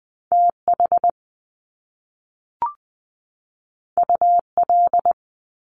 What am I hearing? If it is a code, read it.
Morse code, heard as a keyed tone, about 20 wpm, 700 Hz. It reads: T H U L
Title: Top Two Letter Combinations in Words 20wpm